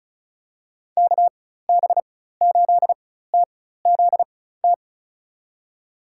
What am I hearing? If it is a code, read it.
Morse code, heard as a keyed tone, about 34 wpm, spaced as for 16 wpm, 700 Hz. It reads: KB8TZT
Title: USA Call Signs - 6 Characters - 4x Character Spacing - 35wpm